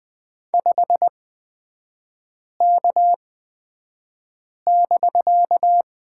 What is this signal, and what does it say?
Morse code, heard as a keyed tone, about 20 wpm, spaced as for 5 wpm, 700 Hz. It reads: 5K<BK>
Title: 250 Lesson 40e - BK - Three Characters 20wpm